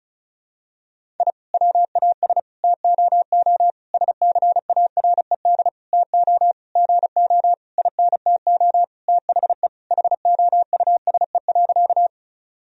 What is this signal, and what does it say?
Morse code, heard as a keyed tone, about 36 wpm, 700 Hz. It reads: I WAS TOO SCARED TO GO INTO THE HOUSE.
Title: Sentences from Top 1000 Words - Review Part 4 - 35wpm